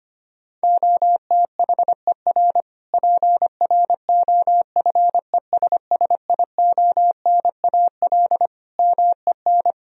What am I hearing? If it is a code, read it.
Morse code, heard as a keyed tone, about 25 wpm, 700 Hz. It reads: OTHER PROFESSIONAL MEN
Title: Sentences from Top 1000 Words - Review Part 4 - 25wpm